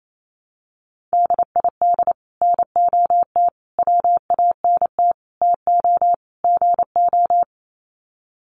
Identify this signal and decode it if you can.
Morse code, heard as a keyed tone, about 28 wpm, 700 Hz. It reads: DID NOT WANT TO GO
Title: Sets of 5 Words 28wpm